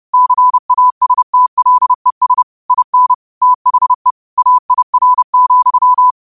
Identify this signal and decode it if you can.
Morse code, heard as a keyed tone, about 30 wpm, 1000 Hz. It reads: CASTLES IN THE AIR,
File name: qotdfiles / 30wpm